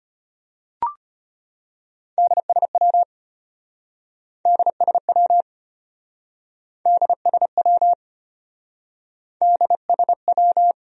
Morse code, heard as a keyed tone, about 34 wpm, 700 Hz. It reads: DSW DSW DSW DSW